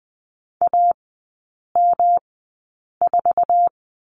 Morse code, heard as a keyed tone, about 20 wpm, spaced as for 8 wpm, 700 Hz. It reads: AM4
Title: 107 Lesson 18e - Letter m - Three Characters 20wpm